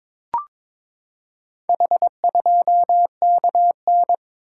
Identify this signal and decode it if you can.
Morse code, heard as a keyed tone, about 22 wpm, 700 Hz. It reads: H2KN
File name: practice / 22wpm / Four Letter-Number Combinations - 22wpm